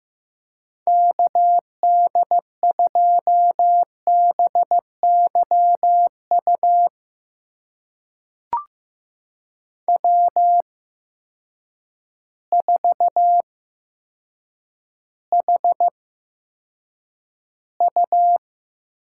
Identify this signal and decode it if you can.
Morse code, heard as a keyed tone, about 15 wpm, spaced as for 8 wpm, 700 Hz. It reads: KD2BYU W4HU
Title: USA Call Signs - 8x Character Spacing - 15wpm